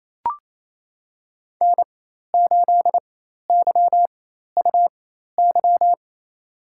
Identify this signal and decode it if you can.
Morse code, heard as a keyed tone, about 27 wpm, spaced as for 13 wpm, 700 Hz. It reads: N8YUY
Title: USA Call Signs - 5 Characters - 4x Character Spacing - 28wpm